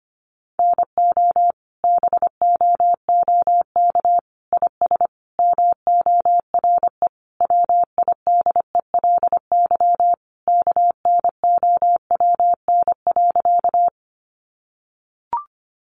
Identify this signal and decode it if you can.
Morse code, heard as a keyed tone, about 25 wpm, 700 Hz. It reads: NO BOOK IS MORE WIDELY KNOWN.
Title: Sentences from Top 1200 Words 25wpm